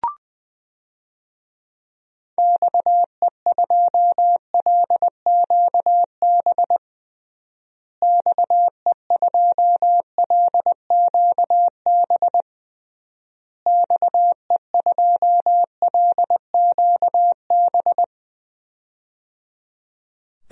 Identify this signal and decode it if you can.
Morse code, heard as a keyed tone, about 20 wpm, 700 Hz. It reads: XE2LQB XE2LQB XE2LQB
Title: DXCC-100 Call Signs - 6 Characters - Repeated 3x - 20wpm